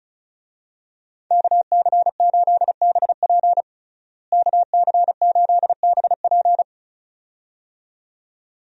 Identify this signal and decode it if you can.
Morse code, heard as a keyed tone, about 34 wpm, 700 Hz. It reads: KC8BP KC8BP
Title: USA Call Signs - 5 Characters - Repeated 2x - 35wpm